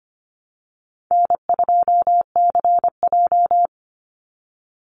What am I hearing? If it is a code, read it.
Morse code, heard as a keyed tone, about 25 wpm, 700 Hz. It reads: N2CJ
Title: Easy Call Signs - 25wpm